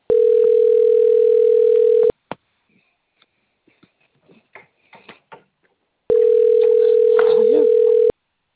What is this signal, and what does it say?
Ringback tone (North America), 440 + 480 Hz.
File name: out- - - - - .17041